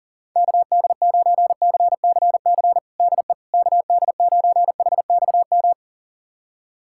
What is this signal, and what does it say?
Morse code, heard as a keyed tone, about 37 wpm, 690 Hz. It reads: KD9CCC DE KD9HXM
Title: USA Callsign DE Callsign - 6 Characters - 40wpm